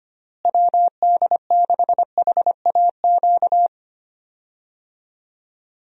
Morse code, heard as a keyed tone, about 25 wpm, 700 Hz. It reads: WD6HAQ
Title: USA Call Signs - 6 Characters - Repeated 2x - 25wpm